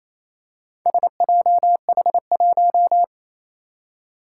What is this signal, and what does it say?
Morse code, heard as a keyed tone, about 27 wpm, 700 Hz. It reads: SJH1
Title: Four Challenging Letter-Number Combinations 28wpm